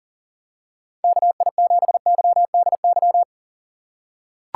Morse code, heard as a keyed tone, about 40 wpm, 700 Hz. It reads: KI7YDY
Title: USA Call Signs - 2x Character Spacing - 40wpm